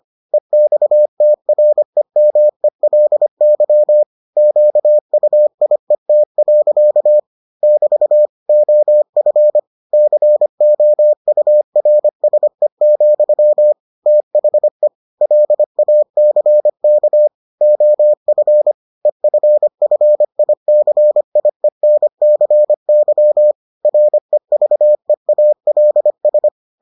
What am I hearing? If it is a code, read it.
Morse code, heard as a keyed tone, about 25 wpm, 600 Hz. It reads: EXTREMELY QUIET. <BT> OF COURSE, THE LACK OF EFFICIENCY REVEALS